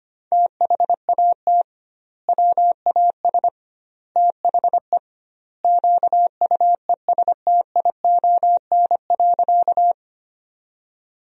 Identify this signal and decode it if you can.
Morse code, heard as a keyed tone, about 25 wpm, 700 Hz. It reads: THAT WAS THE QUESTION.
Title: 2x Word-Spacing - Sentences from Top 400 Words - 25wpm